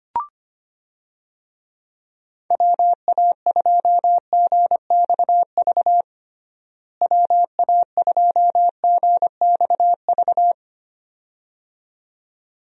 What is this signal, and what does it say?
Morse code, heard as a keyed tone, about 25 wpm, 700 Hz. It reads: WA2GXV WA2GXV
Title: USA Call Signs - 6 Characters - Repeated 2x - 25wpm